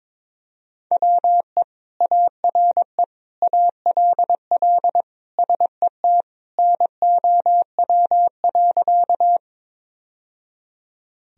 Morse code, heard as a keyed tone, about 22 wpm, 700 Hz. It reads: WE ARE ALL SET NOW.
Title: Sentences from Top 1000 Words - Review Part 1 - 22wpm